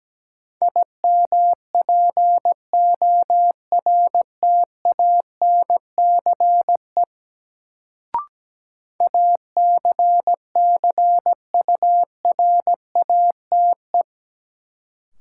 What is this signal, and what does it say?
Morse code, heard as a keyed tone, about 17 wpm, 700 Hz. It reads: IMPORTANCE ACCURATE
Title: Sentences from Top 1200 Words - New Word Drill Alt - 17wpm